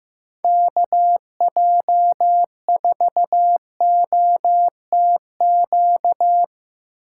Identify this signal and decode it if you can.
Morse code, heard as a keyed tone, about 16 wpm, 700 Hz. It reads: KJ4OTQ
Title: USA Call Signs - 6 Characters - 4x Character Spacing - 15wpm